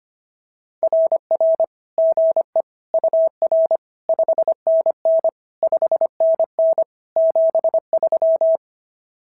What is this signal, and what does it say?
Morse code, heard as a keyed tone, about 25 wpm, 650 Hz. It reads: RR GE UR 5NN 5NN 73
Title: SOTA QSO Practice 25wpm